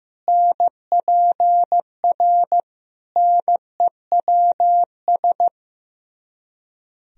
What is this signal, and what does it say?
Morse code, heard as a keyed tone, about 15 wpm, 700 Hz. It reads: NPR NEWS
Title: Speed-Racing - Sets of 5 Words 15wpm